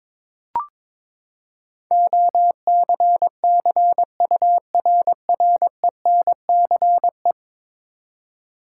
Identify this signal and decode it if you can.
Morse code, heard as a keyed tone, about 22 wpm, 700 Hz. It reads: OCCURRENCE
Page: morse-code-ninja-practice-22wpm